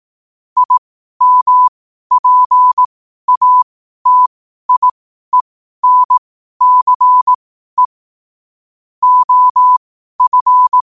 Morse code, heard as a keyed tone, about 18 wpm, spaced as for 12 wpm, 1000 Hz. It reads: IMPATIENCE OF
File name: 12wpm